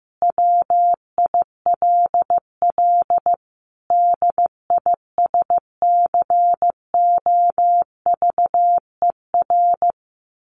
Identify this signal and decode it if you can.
Morse code, heard as a keyed tone, about 15 wpm, 700 Hz. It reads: WILL DISCOVER
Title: Sentences from Top 1100 Words 15wpm